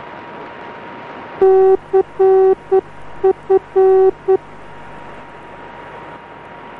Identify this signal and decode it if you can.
Morse code, heard as a keyed tone, about 9 wpm, 380 Hz. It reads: CF